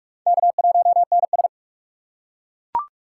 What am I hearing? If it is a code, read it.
Morse code, heard as a keyed tone, about 40 wpm, 700 Hz. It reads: K1NS